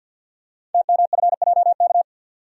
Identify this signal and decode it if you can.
Morse code, heard as a keyed tone, about 51 wpm, 700 Hz. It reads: TMFJK
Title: Code Groups - 5 Letter:Numbers - 50wpm